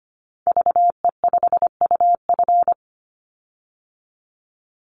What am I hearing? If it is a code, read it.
Morse code, heard as a keyed tone, about 25 wpm, 700 Hz. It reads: VE5UF
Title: Call Signs Repeated 3x - 25wpm